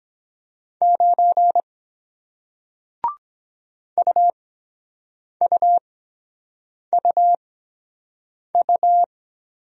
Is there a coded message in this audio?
Yes: Morse code from 0.8 to 9.0 s, heard as a keyed tone, about 23 wpm, spaced as for 7 wpm, 700 Hz. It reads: 9 UUUU